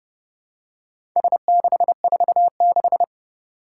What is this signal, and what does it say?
Morse code, heard as a keyed tone, about 29 wpm, 700 Hz. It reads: S646
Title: Four Challenging Letter-Number Combinations 30wpm